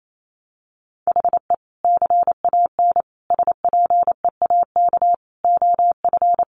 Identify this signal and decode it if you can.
Morse code, heard as a keyed tone, about 28 wpm, 700 Hz. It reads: HE CAN SPEAK OF
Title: Sentences from Top 1100 Words 28wpm